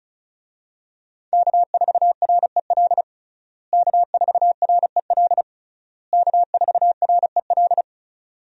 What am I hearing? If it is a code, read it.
Morse code, heard as a keyed tone, about 36 wpm, 700 Hz. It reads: K4REL K4REL K4REL
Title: USA Call Signs - 5 Characters - Repeated 3x - 35wpm